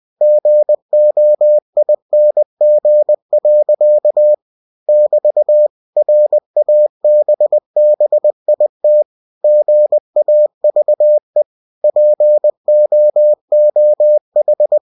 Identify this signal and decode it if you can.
Morse code, heard as a keyed tone, about 20 wpm, 600 Hz. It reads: GOING. <BT> RABBIT GAVE POOH
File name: cw-books / winnie-the-pooh / winnie-20wpm-page-0068